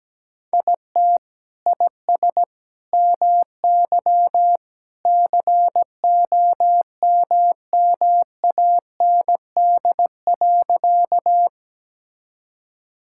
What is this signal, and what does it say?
Morse code, heard as a keyed tone, about 17 wpm, 700 Hz. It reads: IT IS MY COMMAND.
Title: Sentences from Top 1600 Words 17wpm